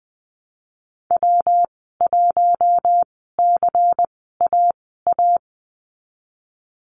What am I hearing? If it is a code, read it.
Morse code, heard as a keyed tone, about 19 wpm, spaced as for 14 wpm, 700 Hz. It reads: W1CAA
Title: 167 Lesson 27g - Number 7 - Call Signs 20wpm